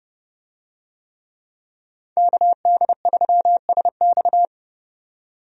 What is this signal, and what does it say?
Morse code, heard as a keyed tone, about 31 wpm, 700 Hz. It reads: KD3SX
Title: USA Call Signs - 5 Characters - 30wpm